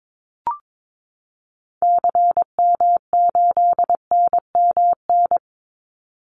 Morse code, heard as a keyed tone, about 22 wpm, 700 Hz. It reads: CM8NMN
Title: Call Signs Repeated 2x - 22wpm